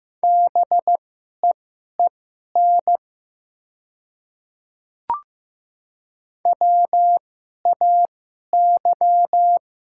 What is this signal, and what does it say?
Morse code, heard as a keyed tone, about 15 wpm, spaced as for 11 wpm, 700 Hz. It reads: BEEN WAY
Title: 2x Character Spacing - Top 200 Words - 15wpm